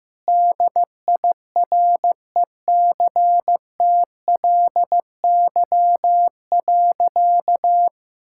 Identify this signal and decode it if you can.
Morse code, heard as a keyed tone, about 15 wpm, 700 Hz. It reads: DIRECTLY.
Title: Sentences from Top 1000 Words - Review Part 5 - 15wpm